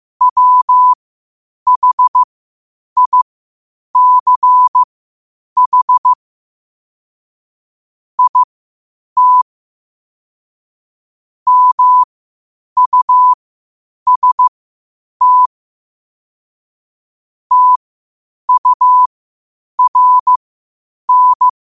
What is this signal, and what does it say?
Morse code, heard as a keyed tone, about 15 wpm, spaced as for 8 wpm, 1000 Hz. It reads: WHICH IT MUST TURN
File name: qotdfiles / 8wpm